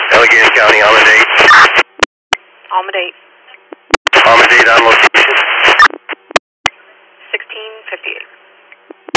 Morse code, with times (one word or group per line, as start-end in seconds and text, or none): none